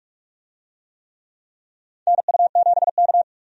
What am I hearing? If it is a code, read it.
Morse code, heard as a keyed tone, about 44 wpm, 700 Hz. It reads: NU7K
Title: USA Call Signs - 4 Characters - 45wpm